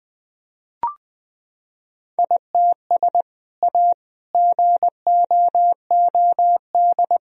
Morse code, heard as a keyed tone, about 21 wpm, 700 Hz. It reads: ITS A GOOD
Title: Sentences from Top 1000 Words - Review Part 4 - 20wpm